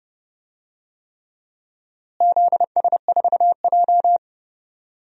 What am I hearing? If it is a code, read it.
Morse code, heard as a keyed tone, about 30 wpm, 700 Hz. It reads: ZS4J
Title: DXCC-100 Call Signs - 4 Characters - Repeated 3x - 30wpm